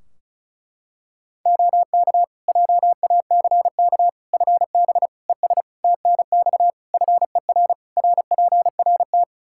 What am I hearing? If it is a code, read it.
Morse code, heard as a keyed tone, about 36 wpm, 700 Hz. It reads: OK JACK FB ES TNX FER RPRT
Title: Practice QSO - Exchange 2 - Info - 35wpm